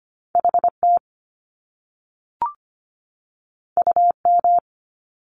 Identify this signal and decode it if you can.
Morse code, heard as a keyed tone, about 25 wpm, 700 Hz. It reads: HT UM
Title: Top Two Letter Combinations in Words 25wpm